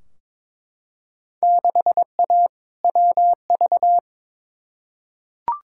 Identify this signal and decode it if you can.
Morse code, heard as a keyed tone, about 22 wpm, 700 Hz. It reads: 6A WV
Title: Field Day Exchanges - 22wpm